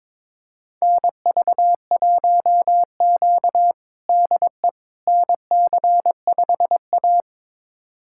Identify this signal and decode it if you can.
Morse code, heard as a keyed tone, about 21 wpm, 700 Hz. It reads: NV1Q DE NC5A
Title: USA Callsign DE Callsign - 4 Characters - 22wpm